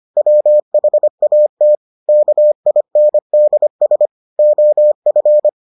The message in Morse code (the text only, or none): WHAT KINDS OF